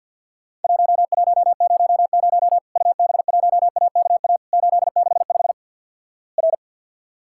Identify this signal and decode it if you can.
Morse code, heard as a keyed tone, about 48 wpm, 700 Hz. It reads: 1100 UB1AKA 865 R
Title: Worked All Europe DX Contest - QTC Exchange - 50wpm